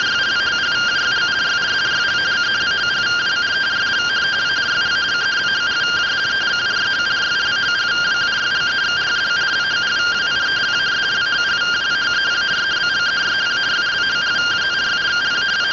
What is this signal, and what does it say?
Two-tone teleprinter signal, about 170 Hz shift.